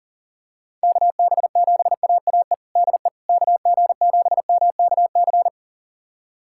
Morse code, heard as a keyed tone, about 40 wpm, 700 Hz. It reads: KB7AAE DE KG7MKC
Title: USA Callsign DE Callsign - 6 Characters - 40wpm